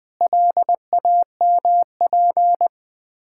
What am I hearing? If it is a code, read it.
Morse code, heard as a keyed tone, about 20 wpm, 690 Hz. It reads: LAMP
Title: Difficult Words with Hints - 20wpm